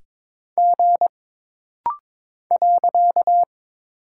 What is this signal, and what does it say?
Morse code, heard as a keyed tone, about 22 wpm, 700 Hz. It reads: G .
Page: This rapid-fire timing is geared to push your ICR (Instant Character Recognition) skills.